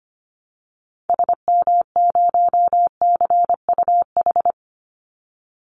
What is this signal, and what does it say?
Morse code, heard as a keyed tone, about 25 wpm, 700 Hz. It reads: SM0CUH
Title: Call Signs Repeated 2x - 25wpm